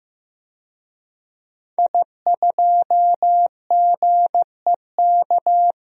Morse code, heard as a keyed tone, about 15 wpm, 700 Hz. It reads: I2GEK
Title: DXCC-100 Call Signs - 5 Characters - 15wpm